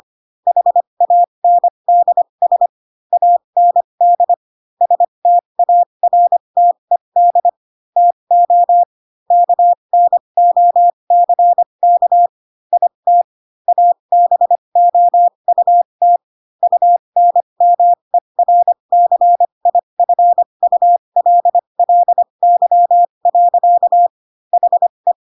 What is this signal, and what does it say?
Morse code, heard as a keyed tone, about 25 wpm, spaced as for 22 wpm, 700 Hz. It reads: HANDS AND STARTED TO KNOCK IT ABOUT UNMERCIFULLY. HE